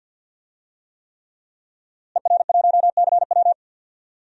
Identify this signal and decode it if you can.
Morse code, heard as a keyed tone, about 50 wpm, 700 Hz. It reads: ER1CW
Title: DXCC-100 Call Signs - 5 Characters - Repeated 3x - 50wpm